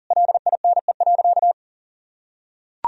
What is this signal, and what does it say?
Morse code, heard as a keyed tone, about 40 wpm, 690 Hz. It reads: LINE.
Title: Sentences from Top 1000 Words - Review Part 4 - 40wpm